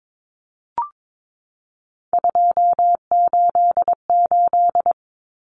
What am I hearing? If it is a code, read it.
Morse code, heard as a keyed tone, about 22 wpm, 700 Hz. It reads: 288
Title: Common Cut Numbers - 3-Digit 22wpm